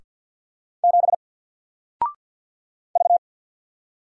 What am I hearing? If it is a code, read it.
Morse code, heard as a keyed tone, about 50 wpm, 700 Hz. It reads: 7 V
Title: Single Letter-Number Alt 50wpm